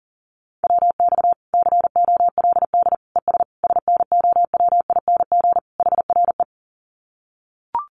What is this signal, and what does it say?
Morse code, heard as a keyed tone, about 40 wpm, 700 Hz. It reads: WX COLD ES SNOWING HRE
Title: Common QSO Phrases 40wpm